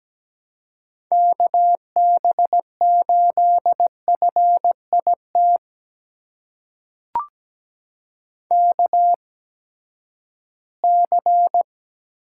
Morse code, heard as a keyed tone, about 17 wpm, spaced as for 12 wpm, 700 Hz. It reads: KB8FIT KC
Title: USA Call Signs - 6 Characters - 8x Character Spacing - 17wpm